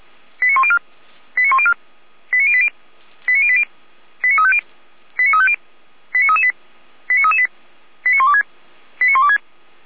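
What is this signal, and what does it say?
Five-tone selective call (ZVEI), heard as quick runs of five tones, tones of about 70 ms. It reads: 89104 89104 89080 89080 89350 89350 89338 89338 89126 89126
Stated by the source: ZVEI five-tone selective call received on SDR.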